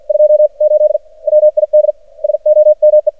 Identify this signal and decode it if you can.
Morse code, heard as a keyed tone, about 50 wpm, spaced as for 44 wpm, 610 Hz. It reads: 28 WID SOME
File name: archived / CRSnet